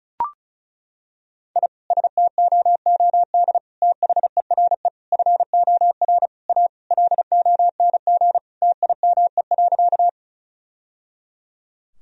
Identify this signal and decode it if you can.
Morse code, heard as a keyed tone, about 36 wpm, 700 Hz. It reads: I STOOD THERE FOR A LONG TIME.
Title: Sentences from Top 1000 Words - Review Part 2 - 35wpm